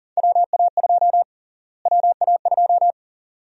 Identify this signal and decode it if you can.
Morse code, heard as a keyed tone, about 40 wpm, 700 Hz. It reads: WA2 WA2
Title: USA Call Sign Prefixes - Repeated 2x - 40wpm